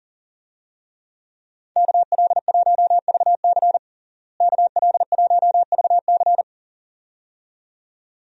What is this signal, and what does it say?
Morse code, heard as a keyed tone, about 40 wpm, 700 Hz. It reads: KL1VC KL1VC
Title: USA Call Signs - 5 Characters - Repeated 2x - 40wpm